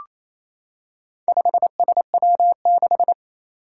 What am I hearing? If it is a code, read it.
Morse code, heard as a keyed tone, about 27 wpm, 700 Hz. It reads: 5SW6
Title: Four Challenging Letter-Number Combinations 28wpm